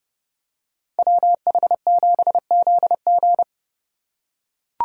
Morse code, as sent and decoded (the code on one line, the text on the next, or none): .-- .... --... --.. --.
WH7ZG